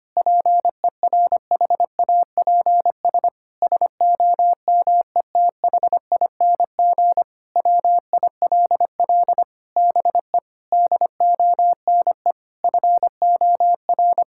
Morse code, heard as a keyed tone, about 25 wpm, 700 Hz. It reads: PERHAPS SOMETHING WILL BE DONE FOR
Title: Sentences from Top 1600 Words 25wpm